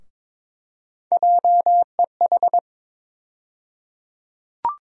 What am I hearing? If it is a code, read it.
Morse code, heard as a keyed tone, about 21 wpm, 700 Hz. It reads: JEH
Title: USA Call Sign Postfixes - 3 Letters 22wpm